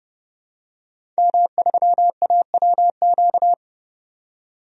Morse code, heard as a keyed tone, about 30 wpm, 700 Hz. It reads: M3AWQ
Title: DXCC-100 Call Signs - 5 Characters - 30wpm